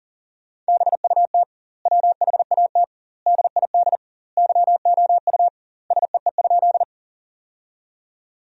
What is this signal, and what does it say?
Morse code, heard as a keyed tone, about 40 wpm, 700 Hz. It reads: BUT WHAT DID YOU SEE?
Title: 2x Word-Spacing - Sentences from Top 100 Words - 40wpm